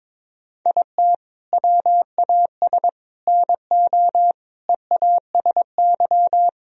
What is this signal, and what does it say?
Morse code, heard as a keyed tone, about 22 wpm, 700 Hz. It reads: IT WAS NO EASY
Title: Sentences from Top 1000 Words - Review Part 3 - 22wpm